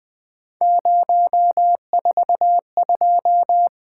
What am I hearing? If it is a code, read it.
Morse code, heard as a keyed tone, about 20 wpm, 700 Hz. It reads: 042
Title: Three Letter-Number Combinations - 20wpm